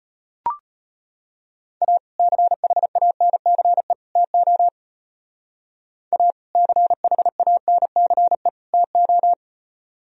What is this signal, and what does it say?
Morse code, heard as a keyed tone, about 31 wpm, 700 Hz. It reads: A CHANCE TO A CHANCE TO
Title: Speed-Racing - Sets of 3 Words 25wpm